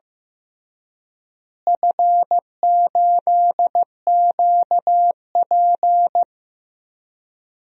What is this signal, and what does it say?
Morse code, heard as a keyed tone, about 15 wpm, 700 Hz. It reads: F8QP
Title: DXCC-100 Call Signs - 4 Characters - Repeated 3x - 15wpm